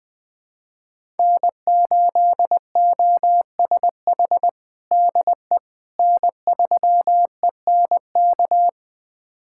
Morse code, heard as a keyed tone, about 20 wpm, 700 Hz. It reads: N8OSH DE N3ENK
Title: USA Callsign DE Callsign - 5 Characters - 20wpm